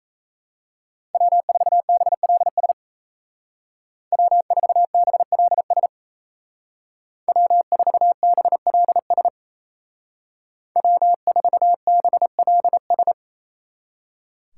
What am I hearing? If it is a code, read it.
Morse code, heard as a keyed tone, about 36 wpm, 700 Hz. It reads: W4BLS W4BLS W4BLS W4BLS